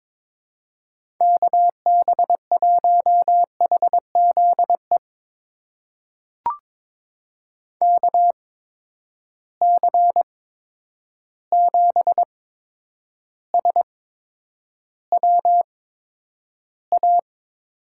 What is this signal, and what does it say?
Morse code, heard as a keyed tone, about 22 wpm, spaced as for 9 wpm, 700 Hz. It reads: KB1HZE KC7SWA
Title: USA Call Signs - 6 Characters - 8x Character Spacing - 22wpm